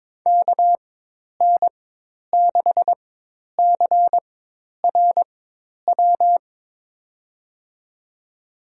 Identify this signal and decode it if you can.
Morse code, heard as a keyed tone, about 22 wpm, spaced as for 10 wpm, 700 Hz. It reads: KN6CRW
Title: USA Call Signs - 4x Character Spacing - 22wpm